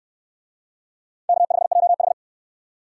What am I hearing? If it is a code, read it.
Morse code, heard as a keyed tone, about 69 wpm, 700 Hz. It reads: K5JH